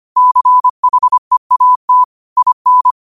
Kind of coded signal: Morse code